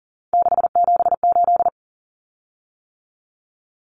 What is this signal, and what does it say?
Morse code, heard as a keyed tone, about 40 wpm, 700 Hz. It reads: 678
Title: Common Cut Numbers - 3-Digit 40wpm